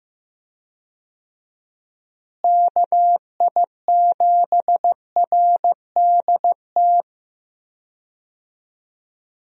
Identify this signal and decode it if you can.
Morse code, heard as a keyed tone, about 15 wpm, 700 Hz. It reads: KI7RDT
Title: USA Call Signs - 6 Characters - Repeated 2x - 15wpm